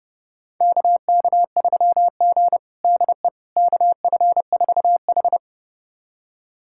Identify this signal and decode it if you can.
Morse code, heard as a keyed tone, about 30 wpm, 700 Hz. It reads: KK3G DE KF4H
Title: USA Callsign DE Callsign - 4 Characters - 30wpm